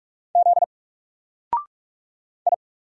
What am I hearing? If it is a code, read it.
Morse code, heard as a keyed tone, about 44 wpm, 700 Hz. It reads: Z I
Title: Single Letter-Number Alt 45wpm